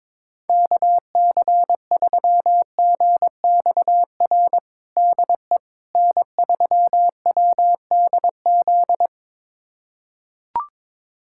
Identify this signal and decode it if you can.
Morse code, heard as a keyed tone, about 22 wpm, 700 Hz. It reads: KC3GXR DE N3WDZ
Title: USA Callsign DE Callsign 22wpm